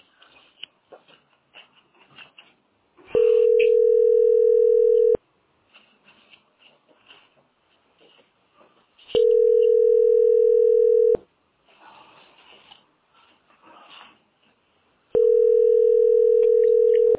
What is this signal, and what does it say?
Ringback tone (North America), 440 + 480 Hz.